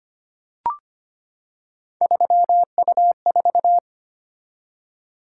Morse code, heard as a keyed tone, about 25 wpm, 700 Hz. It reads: 3U4
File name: Three Letter-Number Combinations - 25wpm